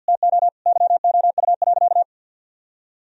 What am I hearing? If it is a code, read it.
Morse code, heard as a keyed tone, about 46 wpm, 700 Hz. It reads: TO YOU.
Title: Sentences from Top 1000 Words - Review Part 2 - 50wpm